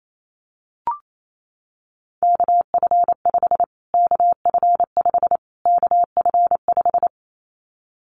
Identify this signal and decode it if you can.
Morse code, heard as a keyed tone, about 28 wpm, 700 Hz. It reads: KF5 KF5 KF5
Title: Callsign Prefixes for Top 25 DXCC - Repeated 3x - 28wpm